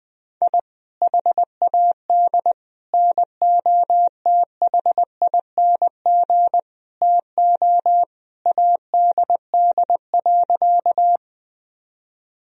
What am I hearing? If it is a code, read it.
Morse code, heard as a keyed tone, about 20 wpm, 700 Hz. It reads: I HAD NOTHING TO ADD.
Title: Sentences from Top 1000 Words - Review Part 1 - 20wpm